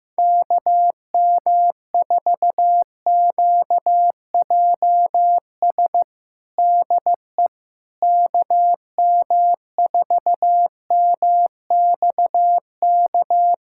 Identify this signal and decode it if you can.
Morse code, heard as a keyed tone, about 15 wpm, 700 Hz. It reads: KM4QJS DE KM4MXK
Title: USA Callsign DE Callsign - 6 Characters - 15wpm